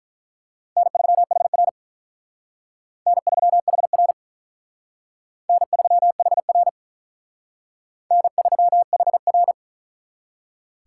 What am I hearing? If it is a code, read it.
Morse code, heard as a keyed tone, about 43 wpm, 700 Hz. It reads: N3HR N3HR N3HR N3HR
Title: Speed Racing - USA 1x2 Call Signs 35wpm